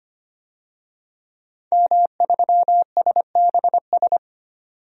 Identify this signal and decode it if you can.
Morse code, heard as a keyed tone, about 25 wpm, 700 Hz. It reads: M3SBS
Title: DXCC-100 Call Signs - 5 Characters - 25wpm